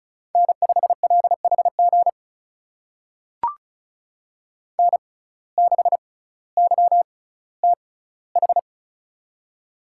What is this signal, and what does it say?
Morse code, heard as a keyed tone, about 34 wpm, spaced as for 18 wpm, 700 Hz. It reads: N5LHG N6YTH
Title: USA Call Signs - 5 Characters - 6x Character Spacing - 35wpm